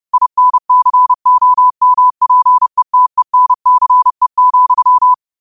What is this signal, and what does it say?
Morse code, heard as a keyed tone, about 30 wpm, 1000 Hz. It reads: INCOMPETENCE,